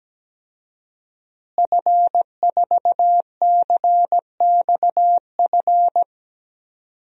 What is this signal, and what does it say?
Morse code, heard as a keyed tone, about 17 wpm, 700 Hz. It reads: F4CXF
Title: DXCC-100 Call Signs - 5 Characters - Repeated 3x - 17wpm